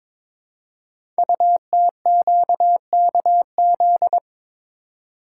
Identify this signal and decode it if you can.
Morse code, heard as a keyed tone, about 22 wpm, 700 Hz. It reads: UTQKZ
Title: Code Groups - 5 Letters - 22wpm